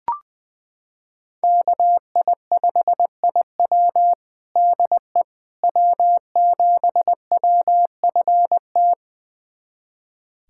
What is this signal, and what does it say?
Morse code, heard as a keyed tone, about 20 wpm, 700 Hz. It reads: KI5IW DE W7WFT
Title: USA Callsign DE Callsign 20wpm